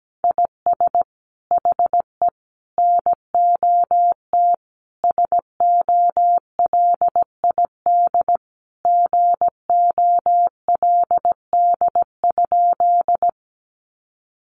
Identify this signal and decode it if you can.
Morse code, heard as a keyed tone, about 17 wpm, 700 Hz. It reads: IS HE NOT SOLID GOLD?
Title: Sentences from Top 1500 Words 17wpm